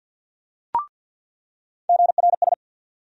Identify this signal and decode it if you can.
Morse code, heard as a keyed tone, about 47 wpm, 700 Hz. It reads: GRS
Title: USA Call Sign Postfixes - 3 Letters 50wpm